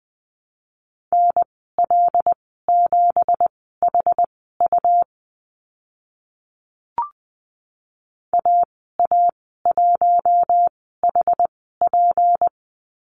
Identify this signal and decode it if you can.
Morse code, heard as a keyed tone, about 19 wpm, spaced as for 14 wpm, 700 Hz. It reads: NL7HU AA1HP
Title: 179 Lesson 29g - Slash - Call Signs 20wpm